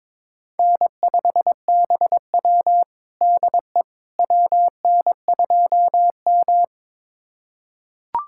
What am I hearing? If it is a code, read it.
Morse code, heard as a keyed tone, about 22 wpm, 700 Hz. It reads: N5BW DE WN2M
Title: USA Callsign DE Callsign - 4 Characters - 22wpm